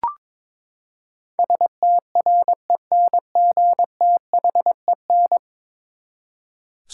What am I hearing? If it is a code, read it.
Morse code, heard as a keyed tone, about 22 wpm, 700 Hz. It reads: STRENGTHEN
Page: morse-code-ninja-practice-22wpm